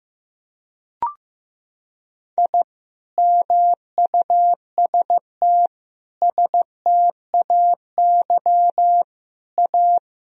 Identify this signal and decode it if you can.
Morse code, heard as a keyed tone, about 15 wpm, 700 Hz. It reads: I MUST STAY A
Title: Sentences from Top 1000 Words - Review Part 1 - 15wpm